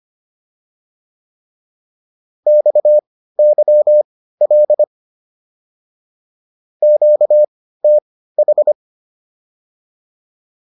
Morse code, heard as a keyed tone, about 25 wpm, spaced as for 15 wpm, 600 Hz. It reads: XYL QTH